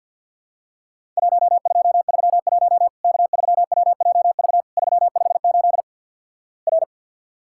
Morse code, heard as a keyed tone, about 53 wpm, 700 Hz. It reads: 1231 K3WJV 358 R
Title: Worked All Europe DX Contest - QTC Exchange - 50wpm